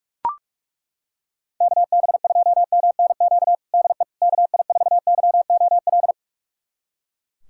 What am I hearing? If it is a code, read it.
Morse code, heard as a keyed tone, about 42 wpm, 700 Hz. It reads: KB2MNQ DE KI4YOL
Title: USA Callsign DE Callsign - 6 Characters - 45wpm